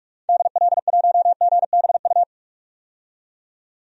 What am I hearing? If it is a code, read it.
Morse code, heard as a keyed tone, about 42 wpm, 700 Hz. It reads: DL1GBU
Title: DXCC-100 Call Signs - 6 Characters - 45wpm